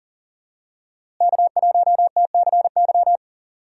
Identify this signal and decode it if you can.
Morse code, heard as a keyed tone, about 40 wpm, 700 Hz. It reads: K1TCY
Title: USA Call Signs - 5 Characters - Repeated 3x - 40wpm